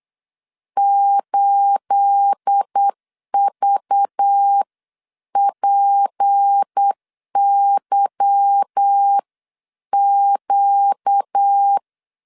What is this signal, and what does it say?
Morse code, heard as a keyed tone, about 8 wpm, spaced as for 7 wpm, 800 Hz. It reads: 8VPYQ
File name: 8wpm